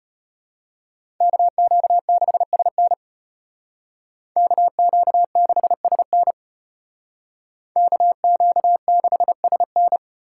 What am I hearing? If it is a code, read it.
Morse code, heard as a keyed tone, about 30 wpm, 700 Hz. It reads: KQ6SN KQ6SN KQ6SN
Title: Speed Racing - USA 2x2 Call Signs 25wpm